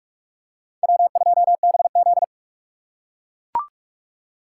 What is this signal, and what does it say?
Morse code, heard as a keyed tone, about 42 wpm, 700 Hz. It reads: W2BZ
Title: USA Call Signs - 4 Characters - 3x Character Spacing - 45wpm